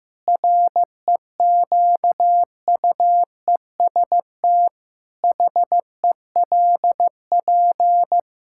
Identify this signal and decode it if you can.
Morse code, heard as a keyed tone, about 15 wpm, 700 Hz. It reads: REQUEST HELP
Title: Pangram Challenge v2 - 15wpm